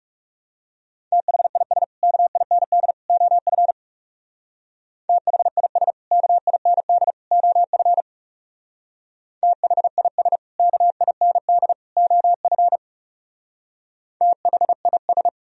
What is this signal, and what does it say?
Morse code, heard as a keyed tone, about 35 wpm, 700 Hz. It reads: THIS KIND OF THIS KIND OF THIS KIND OF THIS
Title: Speed-Racing - Sets of 3 Words 30wpm